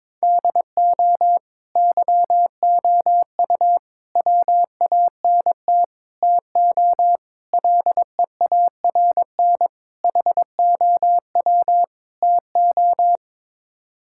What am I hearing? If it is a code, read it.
Morse code, heard as a keyed tone, about 22 wpm, 700 Hz. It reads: DO YOU WANT TO LEARN HOW TO
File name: practice / 22wpm / Sets of 7 Words - 22wpm